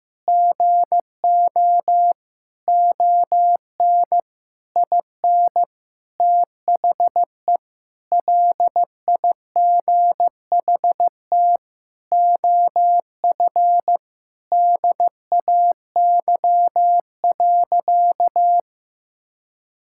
Morse code, heard as a keyed tone, about 15 wpm, 700 Hz. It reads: GO ON IN THE LIGHT OF DAY.
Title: Sentences from Top 1000 Words - Review Part 5 - 15wpm